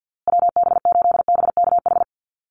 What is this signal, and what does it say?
Morse code, heard as a keyed tone, about 50 wpm, 700 Hz. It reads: WB8BKH
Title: USA Call Signs Repeated 3x 50wpm